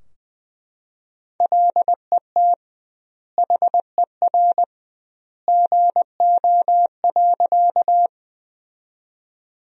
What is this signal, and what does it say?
Morse code, heard as a keyed tone, about 21 wpm, 700 Hz. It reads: LET HER GO.
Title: 2x Word-Spacing - Sentences from Top 300 Words - 20wpm